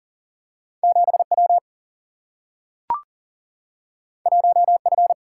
Morse code, heard as a keyed tone, about 40 wpm, 700 Hz. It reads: 7W 1F
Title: Single-Letter Single-Number 40wpm